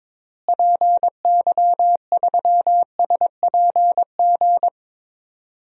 Morse code, heard as a keyed tone, about 21 wpm, 700 Hz. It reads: PY3SPG
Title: DXCC-100 Call Signs - 6 Characters - Repeated 2x - 22wpm